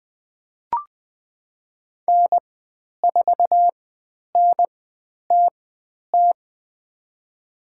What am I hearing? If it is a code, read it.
Morse code, heard as a keyed tone, about 20 wpm, spaced as for 10 wpm, 700 Hz. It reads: N4NTT
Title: 095 Lesson 16g - Letter c - Call Signs 20wpm